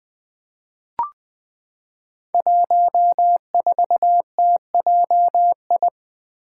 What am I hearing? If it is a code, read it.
Morse code, heard as a keyed tone, about 20 wpm, 700 Hz. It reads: 14TJI
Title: Five Challenging Letter-Number Combinations 20wpm